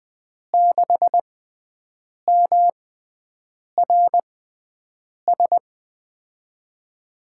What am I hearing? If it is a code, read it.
Morse code, heard as a keyed tone, about 20 wpm, spaced as for 7 wpm, 700 Hz. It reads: 6MRS
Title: USA Call Signs - 6 Characters - 6x Character Spacing - 20wpm